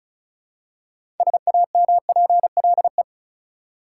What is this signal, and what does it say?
Morse code, heard as a keyed tone, about 36 wpm, 700 Hz. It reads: SAMPLE
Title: Six Letter Words - 35wpm